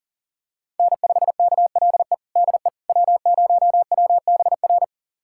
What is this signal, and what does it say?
Morse code, heard as a keyed tone, about 40 wpm, 700 Hz. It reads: N5KLE DE W0WBR
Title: USA Callsign DE Callsign - 5 Characters - 40wpm